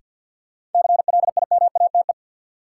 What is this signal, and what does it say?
Morse code, heard as a keyed tone, about 49 wpm, 700 Hz. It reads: CLIMATE
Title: Sentences from Top 1600 Words - New Word Drill Alt - 50wpm